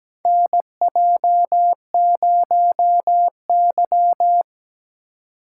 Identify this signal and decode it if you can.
Morse code, heard as a keyed tone, about 17 wpm, 700 Hz. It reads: NJ0Y
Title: USA Call Signs - 4 Characters - 2x Character Spacing - 17wpm